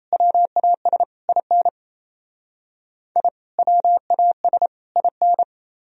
Morse code, heard as a keyed tone, about 36 wpm, 700 Hz. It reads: WAS IN I WAS IN